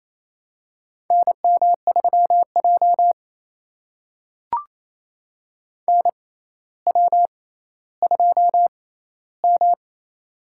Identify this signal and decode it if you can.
Morse code, heard as a keyed tone, about 29 wpm, spaced as for 15 wpm, 700 Hz. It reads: NM3J NW2M
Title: USA Call Signs - 4 Characters - 6x Character Spacing - 28wpm